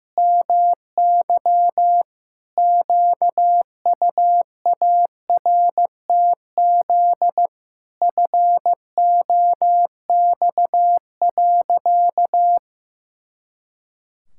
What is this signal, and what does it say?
Morse code, heard as a keyed tone, about 16 wpm, 700 Hz. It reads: MY QUARTZ FOX.
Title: Pangram Challenge v2 - 15wpm